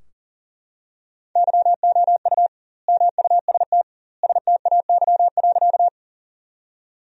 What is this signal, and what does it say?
Morse code, heard as a keyed tone, about 37 wpm, 700 Hz. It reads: YOU MUST STAY.
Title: 2x Word-Spacing - Sentences from Top 500 Words - 40wpm